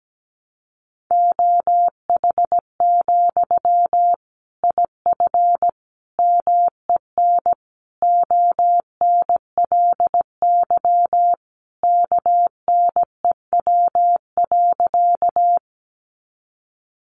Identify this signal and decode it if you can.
Morse code, heard as a keyed tone, about 17 wpm, 700 Hz. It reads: OH, IF MEN ONLY KNEW.
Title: Sentences from Top 500 Words 17wpm